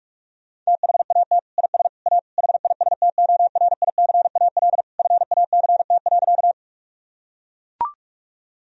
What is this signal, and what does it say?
Morse code, heard as a keyed tone, about 41 wpm, 700 Hz. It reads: THAT IS A HISTORICAL FACT.
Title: Sentences from Top 1000 Words - Review Part 4 - 45wpm